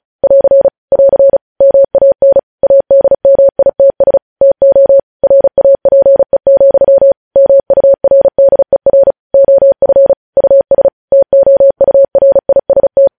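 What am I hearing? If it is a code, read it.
Morse code, heard as a keyed tone, about 36 wpm, 550 Hz. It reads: <AR> <AR> MAN ADMITS TO RAPE, MURDER OF US TOURIST
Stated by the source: Morse code news headlines sent at 35 words per minute.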